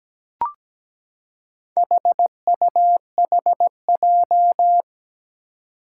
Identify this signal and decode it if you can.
Morse code, heard as a keyed tone, about 17 wpm, 700 Hz. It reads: HUHJ
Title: Four Challenging Letter-Number Combinations 17wpm